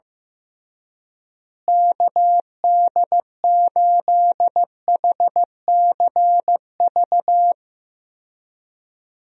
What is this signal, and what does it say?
Morse code, heard as a keyed tone, about 16 wpm, 700 Hz. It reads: KD8HCV
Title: USA Call Signs - 6 Characters - Repeated 2x - 15wpm